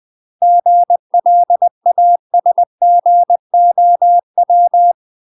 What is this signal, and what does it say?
Morse code, heard as a keyed tone, about 20 wpm, 700 Hz. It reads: GLASGOW
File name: UK MASTER QSOs TRAINING / 20WPM / UK MASTER QSOs 0055